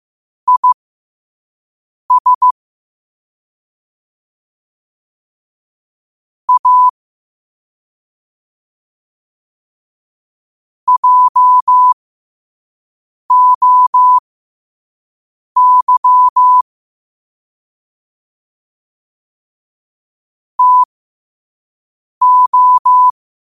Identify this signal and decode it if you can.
Morse code, heard as a keyed tone, about 15 wpm, spaced as for 5 wpm, 1000 Hz. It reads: IS A JOY TO
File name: qotdfiles / 5wpm